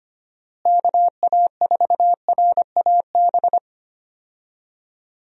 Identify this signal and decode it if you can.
Morse code, heard as a keyed tone, about 25 wpm, 700 Hz. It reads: KA4RAB
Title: USA Call Signs - 6 Characters - Repeated 2x - 25wpm